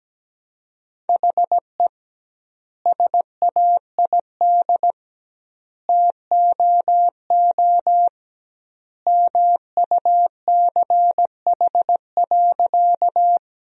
Morse code, heard as a keyed tone, about 17 wpm, 700 Hz. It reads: HE SAID TOO MUCH.
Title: 2x Word-Spacing - Sentences from Top 200 Words - 17wpm